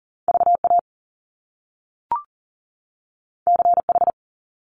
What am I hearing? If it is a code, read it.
Morse code, heard as a keyed tone, about 41 wpm, 700 Hz. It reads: VA CH
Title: Top Two Letter Combinations in Words 40wpm